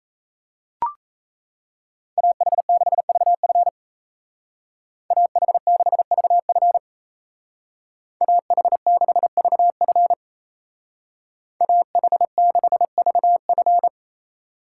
Morse code, heard as a keyed tone, about 34 wpm, 700 Hz. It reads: AH6VF AH6VF AH6VF AH6VF
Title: Speed Racing - USA Call Signs 28wpm